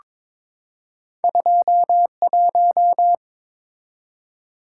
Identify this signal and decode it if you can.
Morse code, heard as a keyed tone, about 22 wpm, 700 Hz. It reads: 21
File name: practice / 22wpm / Two Letter-Number Combinations - 22wpm